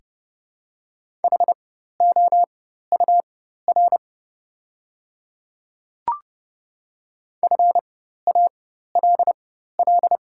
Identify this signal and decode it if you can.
Morse code, heard as a keyed tone, about 32 wpm, spaced as for 14 wpm, 700 Hz. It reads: HOUR FALL
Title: 4x Character Spacing - Top 500 Words - 30wpm